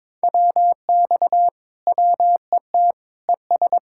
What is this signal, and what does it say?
Morse code, heard as a keyed tone, about 23 wpm, 700 Hz. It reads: WX WET ES
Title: Speed-Racing Common QSO Phrases - 22wpm